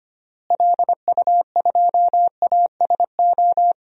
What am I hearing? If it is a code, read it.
Morse code, heard as a keyed tone, about 25 wpm, 700 Hz. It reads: LU2ASO
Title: DXCC-100 Call Signs - 6 Characters - Repeated 3x - 25wpm